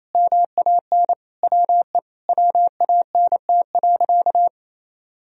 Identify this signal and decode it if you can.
Morse code, heard as a keyed tone, about 28 wpm, 700 Hz. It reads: MAN WE WANT.
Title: Sentences from Top 200 Words and Abbreviations 28wpm